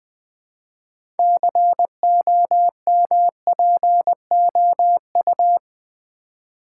Morse code, heard as a keyed tone, about 20 wpm, 700 Hz. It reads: COMPOU
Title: Build A Word - 2 Letters At A Time - 8-Letter Words - 20wpm